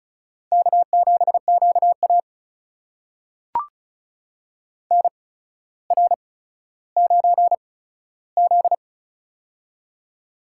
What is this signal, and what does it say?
Morse code, heard as a keyed tone, about 36 wpm, spaced as for 15 wpm, 700 Hz. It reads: K7QA NR9Z
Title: USA Call Signs - 4 Characters - 8x Character Spacing - 35wpm